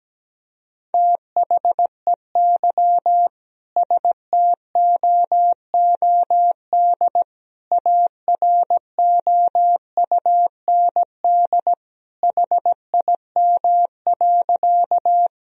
Morse code, heard as a keyed tone, about 17 wpm, 700 Hz. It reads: THEY STOOD AROUND HIM.
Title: Sentences from Top 1000 Words - Review Part 3 - 17wpm